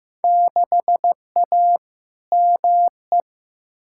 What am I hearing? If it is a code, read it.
Morse code, heard as a keyed tone, about 15 wpm, 700 Hz. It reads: 6A ME
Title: Field Day Exchanges - 15wpm